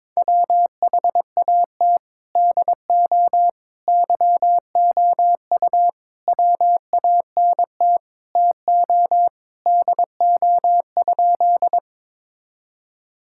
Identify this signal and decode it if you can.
Morse code, heard as a keyed tone, about 22 wpm, 700 Hz. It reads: WHAT DO YOU WANT TO DO?
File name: practice / 22wpm / Sentences with 3-Word Phrases - 22wpm